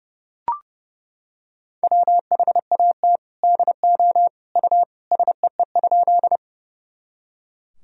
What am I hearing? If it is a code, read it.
Morse code, heard as a keyed tone, about 30 wpm, 700 Hz. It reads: WHAT DO U SEE?
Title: Sentences from Top 100 Words and Abbreviations 30wpm